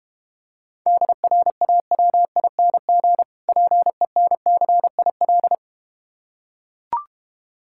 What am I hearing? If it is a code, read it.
Morse code, heard as a keyed tone, about 32 wpm, 700 Hz. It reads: DRAWING PENCIL